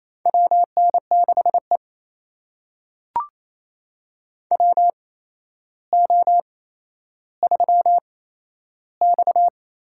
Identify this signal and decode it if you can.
Morse code, heard as a keyed tone, about 28 wpm, spaced as for 12 wpm, 700 Hz. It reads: WN6E WO3X
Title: USA Call Signs - 8x Character Spacing - 28wpm